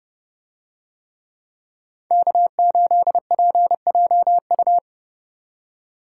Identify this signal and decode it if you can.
Morse code, heard as a keyed tone, about 28 wpm, 700 Hz. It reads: K8PJU